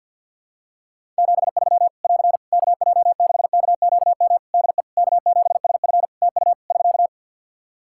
Morse code, heard as a keyed tone, about 52 wpm, 700 Hz. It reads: 73 <AR> KJ6KQM DE K7SF TU <SK>
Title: Practice QSO - Exchange 4 - Ending - 50wpm